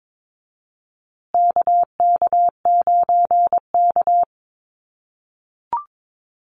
Morse code, heard as a keyed tone, about 22 wpm, 700 Hz. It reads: KK9K
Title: USA Call Signs Repeated 3x 22wpm